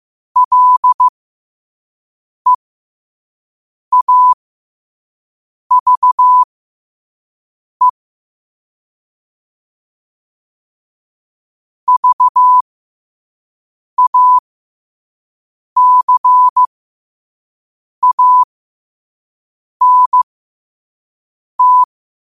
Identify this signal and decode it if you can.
Morse code, heard as a keyed tone, about 15 wpm, spaced as for 5 wpm, 1000 Hz. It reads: LEAVE VACANT
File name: qotdfiles / 5wpm